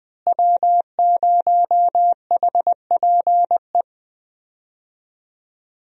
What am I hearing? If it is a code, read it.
Morse code, heard as a keyed tone, about 20 wpm, 700 Hz. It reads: W0HPE
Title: USA Call Signs - 5 Characters - Repeated 3x - 20wpm